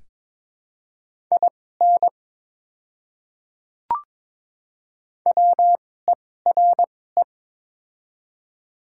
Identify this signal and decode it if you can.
Morse code, heard as a keyed tone, about 22 wpm, spaced as for 16 wpm, 700 Hz. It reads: IN WERE
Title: 2x Character Spacing - Top 100 Words - 22wpm